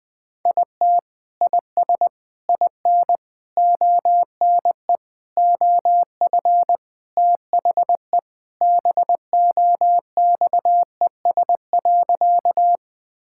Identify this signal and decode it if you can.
Morse code, heard as a keyed tone, about 20 wpm, 700 Hz. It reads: IT IS IN ONE OF THE BOXES.
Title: Sentences with 3-Word Phrases - Spoken Beginning - 20wpm